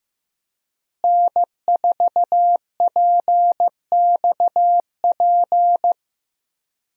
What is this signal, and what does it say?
Morse code, heard as a keyed tone, about 15 wpm, 700 Hz. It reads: N4PXP
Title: USA Call Signs - 5 Characters - Repeated 3x - 15wpm